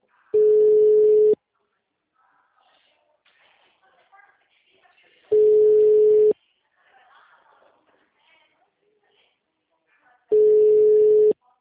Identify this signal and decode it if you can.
Ringback tone (Europe, 425 Hz), with speech.
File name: out- -2016- - - .75